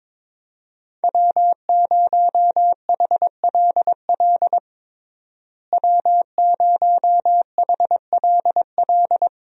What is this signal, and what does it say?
Morse code, heard as a keyed tone, about 21 wpm, 700 Hz. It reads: W0HLL W0HLL
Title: USA Call Signs - 5 Characters - Repeated 2x - 22wpm